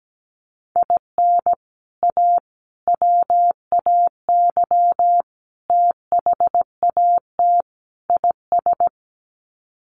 Morse code, heard as a keyed tone, about 17 wpm, 700 Hz. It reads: IN A WAY THAT IS
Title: Sets of 5 Words Encore 17wpm